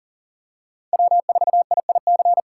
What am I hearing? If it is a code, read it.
Morse code, heard as a keyed tone, about 40 wpm, 700 Hz. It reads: W4IIC
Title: K1USN SST QSO Practice - 40wpm